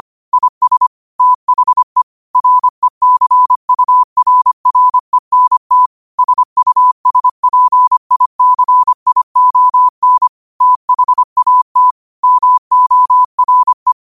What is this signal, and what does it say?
Morse code, heard as a keyed tone, about 25 wpm, 1000 Hz. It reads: IS THE RECURRENT SUSPICION THAT MORE